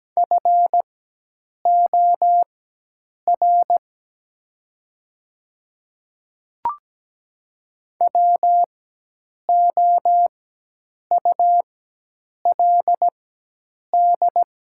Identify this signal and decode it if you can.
Morse code, heard as a keyed tone, about 17 wpm, spaced as for 8 wpm, 700 Hz. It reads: FOR WOULD
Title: 4x Character Spacing - Top 100 Words - 17wpm